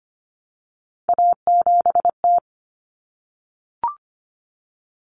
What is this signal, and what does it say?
Morse code, heard as a keyed tone, about 25 wpm, 700 Hz. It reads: A7T
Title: Common Cut Numbers - 3-Digit 25wpm